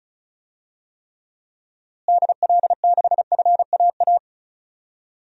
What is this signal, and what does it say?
Morse code, heard as a keyed tone, about 34 wpm, 700 Hz. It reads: DL6FAA
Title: DXCC-100 Call Signs - 6 Characters - Repeated 3x - 35wpm